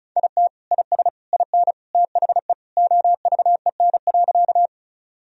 Morse code, heard as a keyed tone, about 34 wpm, 700 Hz. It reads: IT IS IN THE OVEN.
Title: Sentences from Top 1000 Words - Review Part 4 - 35wpm